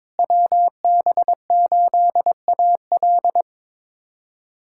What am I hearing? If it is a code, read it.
Morse code, heard as a keyed tone, about 22 wpm, 700 Hz. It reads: WB8AL
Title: USA Callsign DE Callsign 22wpm